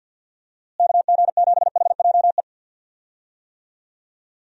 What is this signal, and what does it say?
Morse code, heard as a keyed tone, about 50 wpm, 700 Hz. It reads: KG7HJE
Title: USA Call Signs - 6 Characters - Repeated 3x - 50wpm